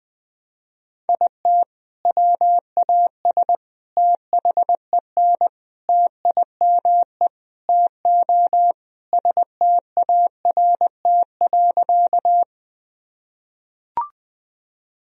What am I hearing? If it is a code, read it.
Morse code, heard as a keyed tone, about 21 wpm, 700 Hz. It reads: IT WAS THEN TIME TO START.
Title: Sentences from Top 1000 Words - Review Part 1 - 20wpm